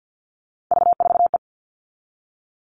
Morse code, heard as a keyed tone, about 50 wpm, 700 Hz. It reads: V4E
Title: Three Challenging Letter-Number Combinations 50wpm